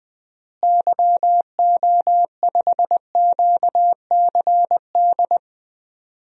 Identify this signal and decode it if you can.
Morse code, heard as a keyed tone, about 20 wpm, 700 Hz. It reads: YO5QCD
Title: DXCC-100 Call Signs - 6 Characters - Repeated 2x - 20wpm